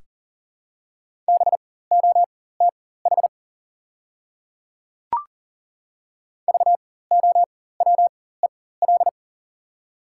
Morse code, heard as a keyed tone, about 40 wpm, spaced as for 19 wpm, 700 Hz. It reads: BOTH VOWEL
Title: 4x Character Spacing - Top 500 Words - 40wpm